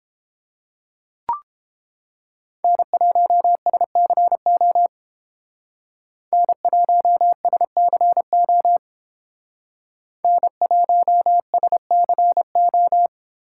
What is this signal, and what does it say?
Morse code, heard as a keyed tone, about 32 wpm, 700 Hz. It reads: N1SCO N1SCO N1SCO